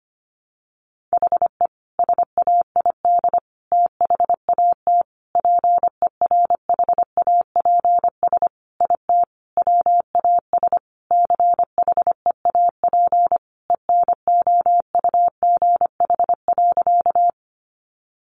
Morse code, heard as a keyed tone, about 25 wpm, 700 Hz. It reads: HE SAID THAT PERHAPS IT WAS CHEAP ENOUGH.
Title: Sentences from Top 1200 Words 25wpm